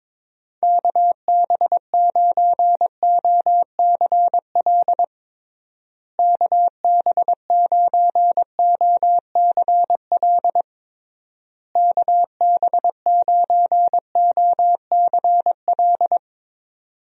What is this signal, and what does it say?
Morse code, heard as a keyed tone, about 22 wpm, 700 Hz. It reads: KB9OCL KB9OCL KB9OCL
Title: USA Call Signs - 6 Characters - Repeated 3x - 22wpm